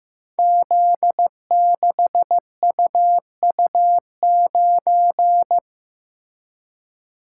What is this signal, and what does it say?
Morse code, heard as a keyed tone, about 15 wpm, 700 Hz. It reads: Z6UU9
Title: Code Groups - 5 Letter:Numbers - 15wpm